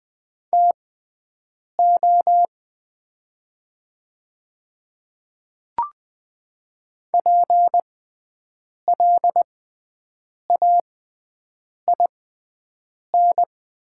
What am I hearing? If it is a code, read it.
Morse code, heard as a keyed tone, about 20 wpm, spaced as for 7 wpm, 700 Hz. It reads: TO PLAIN
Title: 6x Character Spacing - Top 500 Words - 20wpm